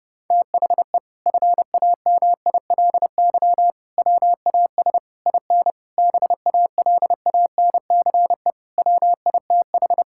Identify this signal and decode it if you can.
Morse code, heard as a keyed tone, about 32 wpm, 700 Hz. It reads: THE FAMILY WAS IN BALANCE WITH
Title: Sentences from Top 1000 Words - Review Part 6 - 30wpm